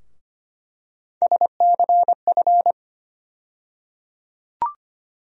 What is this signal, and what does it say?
Morse code, heard as a keyed tone, about 25 wpm, 700 Hz. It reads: SCF
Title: USA Call Sign Postfixes - 3 Letters 25wpm